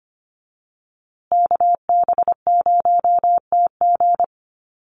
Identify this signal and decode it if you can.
Morse code, heard as a keyed tone, about 25 wpm, 700 Hz. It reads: KB0TG
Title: USA Call Signs Repeated 3x 25wpm